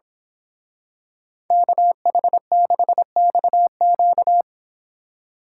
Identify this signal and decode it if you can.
Morse code, heard as a keyed tone, about 25 wpm, 700 Hz. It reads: KH6XQ
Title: Speed Racing - USA Call Signs 22wpm